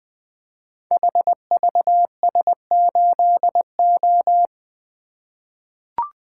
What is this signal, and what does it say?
Morse code, heard as a keyed tone, about 20 wpm, 700 Hz. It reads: HVS8O
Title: Code Groups - 5 Letter:Numbers - 20wpm